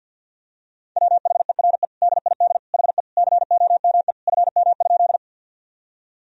Morse code, heard as a keyed tone, about 51 wpm, 700 Hz. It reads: WHERE DID HE COME FM?
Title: Sentences from Top 200 Words and Abbreviations 50wpm